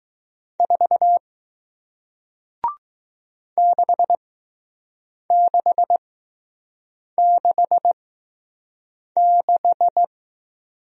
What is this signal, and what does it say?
Morse code, heard as a keyed tone, about 20 wpm, spaced as for 6 wpm, 700 Hz. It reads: 4 6666